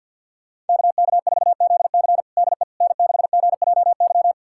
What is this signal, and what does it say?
Morse code, heard as a keyed tone, about 51 wpm, 690 Hz. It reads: KK4ZC DE N6GJY
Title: USA Callsign DE Callsign - 5 Characters - 50wpm